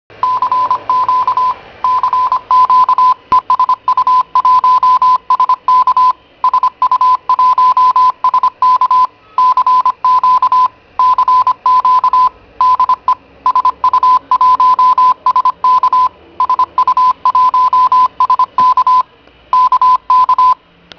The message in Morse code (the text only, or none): CQ CQ E SU1SK SU1SK CQ CQ DE SU1SK SU1SK K K